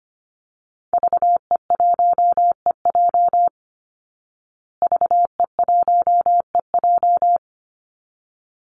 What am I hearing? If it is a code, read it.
Morse code, heard as a keyed tone, about 25 wpm, 700 Hz. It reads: VE1EJ VE1EJ
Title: Call Signs Repeated 3x - 25wpm